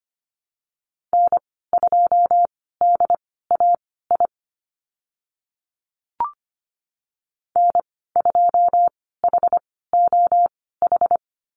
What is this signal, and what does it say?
Morse code, heard as a keyed tone, about 25 wpm, spaced as for 16 wpm, 700 Hz. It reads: N2DAI N2HOH
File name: Lesson 15g - Number 5 - Call Signs 25wpm